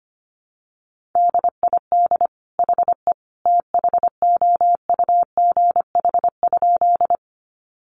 Morse code, heard as a keyed tone, about 25 wpm, 700 Hz. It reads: DID HE THOUGH?
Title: Sentences from Top 400 Words 25wpm